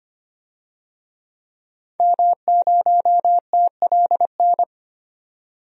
Morse code, heard as a keyed tone, about 25 wpm, 700 Hz. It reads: M0TLN
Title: DXCC-100 Call Signs - 5 Characters - 25wpm